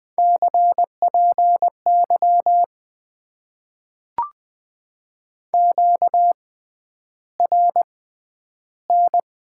Morse code, heard as a keyed tone, about 20 wpm, spaced as for 10 wpm, 700 Hz. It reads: CPY QRN
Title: Top 90 QSO Elements - 6x Character Spacing - 20wpm